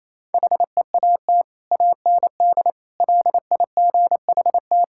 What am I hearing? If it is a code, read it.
Morse code, heard as a keyed tone, about 27 wpm, 700 Hz. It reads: HEAT AND LIGHT
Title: Sentences from Top 1000 Words - Review Part 6 - 28wpm